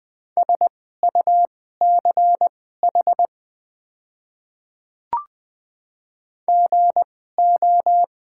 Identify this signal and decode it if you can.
Morse code, heard as a keyed tone, about 20 wpm, spaced as for 14 wpm, 700 Hz. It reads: SUCH GO
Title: 2x Character Spacing - Top 500 Words - 20wpm